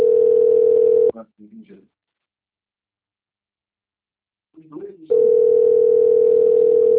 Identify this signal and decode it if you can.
Ringback tone (North America), 440 + 480 Hz, with speech.
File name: out- - - - - .48173